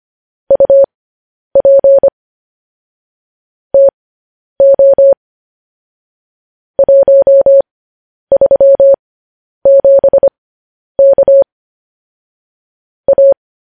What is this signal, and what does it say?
Morse code, heard as a keyed tone, about 25 wpm, spaced as for 10 wpm, 550 Hz. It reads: UP TO 137K A
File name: podcast / News headlines-1x-25x10